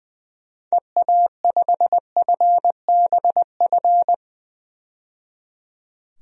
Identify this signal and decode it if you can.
Morse code, heard as a keyed tone, about 20 wpm, 700 Hz. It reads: EA5FBF
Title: DXCC-100 Call Signs - 6 Characters - 20wpm